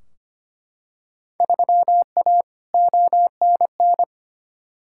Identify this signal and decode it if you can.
Morse code, heard as a keyed tone, about 25 wpm, 700 Hz. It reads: 3A ONN
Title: Field Day Exchanges Encore - 25wpm